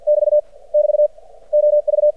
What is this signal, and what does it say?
Morse code, heard as a keyed tone, about 50 wpm, 600 Hz. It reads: <BT> <BT> OV
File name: archived / CRSnet